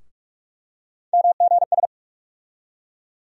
Morse code, heard as a keyed tone, about 45 wpm, 700 Hz. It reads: MGS
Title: USA Call Sign Postfixes - 3 Letters 45wpm